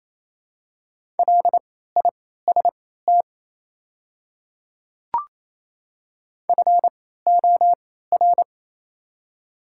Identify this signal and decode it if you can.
Morse code, heard as a keyed tone, about 28 wpm, spaced as for 16 wpm, 700 Hz. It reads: LIST FOR
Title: 3x Character Spacing - Top 500 Words - 28wpm